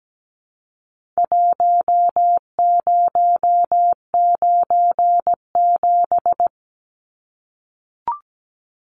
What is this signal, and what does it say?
Morse code, heard as a keyed tone, about 17 wpm, 700 Hz. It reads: 1097
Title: Numbers - Contest Style - 4-Digit 17wpm